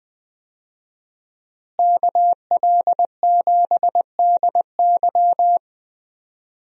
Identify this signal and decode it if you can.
Morse code, heard as a keyed tone, about 20 wpm, 700 Hz. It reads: KL7DY
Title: DXCC-100 Call Signs - 5 Characters - Repeated 2x - 20wpm